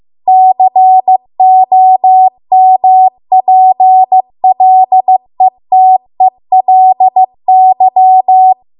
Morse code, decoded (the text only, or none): COMPLETELY